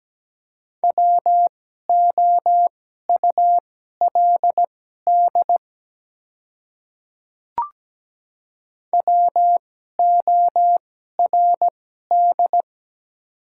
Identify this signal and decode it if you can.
Morse code, heard as a keyed tone, about 17 wpm, spaced as for 12 wpm, 700 Hz. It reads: WOULD WORD
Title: 2x Character Spacing - Top 100 Words - 17wpm